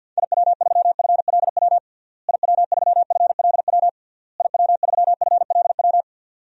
Morse code, heard as a keyed tone, about 51 wpm, 700 Hz. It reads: IW3FLW IW3FLW IW3FLW
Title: DXCC-100 Call Signs - 6 Characters - Repeated 3x - 50wpm